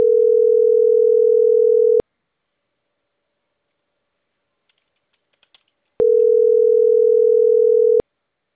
Ringback tone (North America), 440 + 480 Hz.